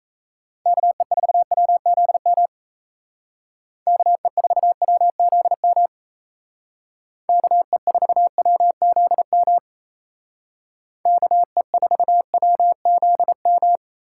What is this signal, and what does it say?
Morse code, heard as a keyed tone, about 32 wpm, 700 Hz. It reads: KE4WZM KE4WZM KE4WZM KE4WZM
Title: Speed Racing - USA Call Signs 28wpm